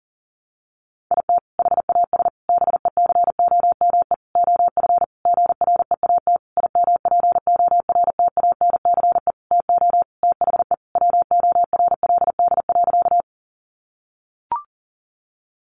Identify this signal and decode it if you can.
Morse code, heard as a keyed tone, about 40 wpm, 700 Hz. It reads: IT HAS BECOME OF GREAT IMPORTANCE TO THE WORLD.
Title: Sentences from Top 1200 Words 40wpm